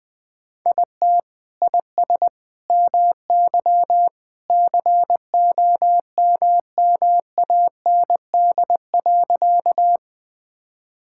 Morse code, heard as a keyed tone, about 20 wpm, 700 Hz. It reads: IT IS MY COMMAND.
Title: Sentences from Top 1600 Words 20wpm